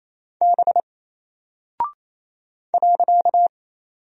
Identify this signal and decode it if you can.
Morse code, heard as a keyed tone, about 27 wpm, 700 Hz. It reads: B .
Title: Single Character (Letter:Number:Punctuation:Prosign) 28wpm